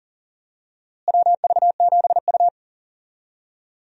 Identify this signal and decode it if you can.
Morse code, heard as a keyed tone, about 40 wpm, 700 Hz. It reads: WV7U
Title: USA Call Signs - 4 Characters - Repeated 3x - 40wpm